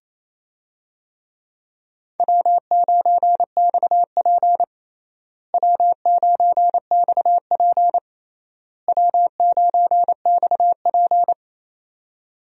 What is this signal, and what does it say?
Morse code, heard as a keyed tone, about 28 wpm, 700 Hz. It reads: W9XP W9XP W9XP
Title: USA Call Signs - 4 Characters - Repeated 3x - 28wpm